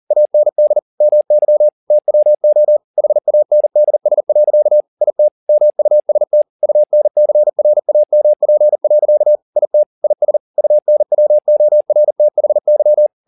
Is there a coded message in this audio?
Yes: Morse code from 0.1 to 13.1 s, heard as a keyed tone, about 40 wpm, 600 Hz. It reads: AND MY TWO HANDS. IT MUST UNCRAMP. IT IS UNWORTHY